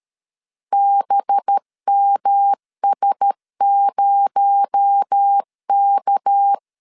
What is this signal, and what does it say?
Morse code, heard as a keyed tone, about 13 wpm, 800 Hz. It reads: BMS0K
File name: CW / DET / 1 6 1500C S B 12wpm 3431